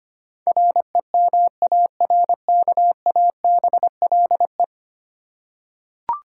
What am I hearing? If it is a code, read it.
Morse code, heard as a keyed tone, about 25 wpm, 700 Hz. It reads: REMARKABLE
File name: 10 Letter Words - First 3 Letters Given - 25wpm